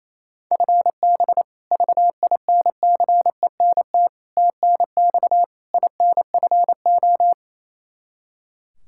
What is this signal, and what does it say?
Morse code, heard as a keyed tone, about 28 wpm, 700 Hz. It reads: FB VINCENT TNX INFO
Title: Practice QSO - Exchange 3 - More Info - 28wpm